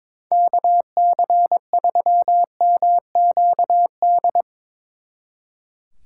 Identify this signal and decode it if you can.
Morse code, heard as a keyed tone, about 24 wpm, 700 Hz. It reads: KC3MQD
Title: Speed Racing - USA Call Signs 22wpm